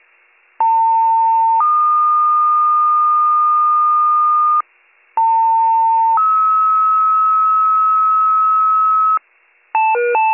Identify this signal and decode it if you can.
Two-tone page, heard as a one-second tone, then a three-second tone, about 900 then 1220 Hz, about 900 then 1290 Hz. It is heard from 0.6 to 9.2 s.